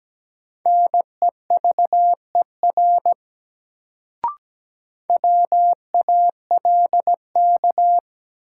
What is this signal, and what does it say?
Morse code, heard as a keyed tone, about 17 wpm, 700 Hz. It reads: NEVER WALK
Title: Sentences from Top 300 Words - New Word Drill - 17wpm